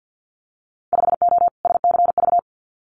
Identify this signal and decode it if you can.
Morse code, heard as a keyed tone, about 50 wpm, 700 Hz. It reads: 5O SCV
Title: Winter Field Day Exchanges Encore 50wpm